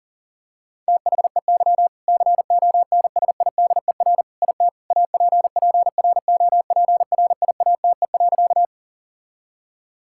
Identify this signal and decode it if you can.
Morse code, heard as a keyed tone, about 37 wpm, 700 Hz. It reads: THEY CONSIDER IT APPROPRIATE.
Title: Sentences from Top 1000 Words - Review Part 5 - 40wpm